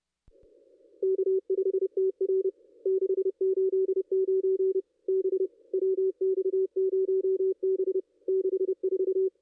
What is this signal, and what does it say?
Morse code, heard as a keyed tone, about 30 wpm, 380 Hz. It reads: K5TR 689 B WX0B 64
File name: contests / ss-cw-k5tr-op-n5rz -0500